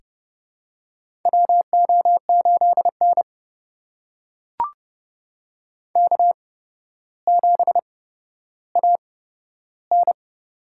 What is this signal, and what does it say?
Morse code, heard as a keyed tone, about 30 wpm, spaced as for 13 wpm, 700 Hz. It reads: WO8N K7AN
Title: USA Call Signs - 4 Characters - 8x Character Spacing - 30wpm